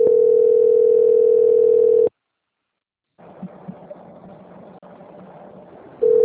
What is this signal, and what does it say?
Ringback tone (North America), 440 + 480 Hz.